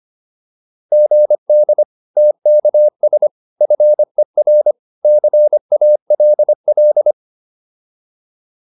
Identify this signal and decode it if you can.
Morse code, heard as a keyed tone, about 25 wpm, 600 Hz. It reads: GD TKS FER CALL